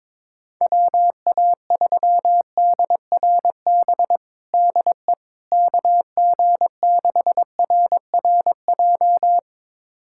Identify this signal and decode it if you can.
Morse code, heard as a keyed tone, about 22 wpm, 700 Hz. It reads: WA3DRB DE KG6RRJ
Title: USA Callsign DE Callsign - 6 Characters - 22wpm